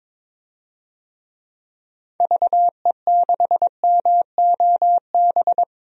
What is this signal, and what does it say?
Morse code, heard as a keyed tone, about 22 wpm, 700 Hz. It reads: VE6MOB
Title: DXCC-100 Call Signs - 6 Characters - 22wpm